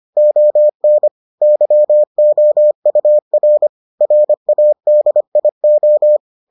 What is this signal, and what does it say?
Morse code, heard as a keyed tone, about 25 wpm, 600 Hz. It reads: ON YOUR RADIO